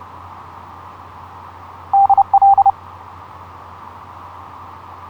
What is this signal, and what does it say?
Morse code, heard as a keyed tone, about 29 wpm, 790 Hz. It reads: DL
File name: zl1byz ha8wx etc 12m cw 06nov2010 0858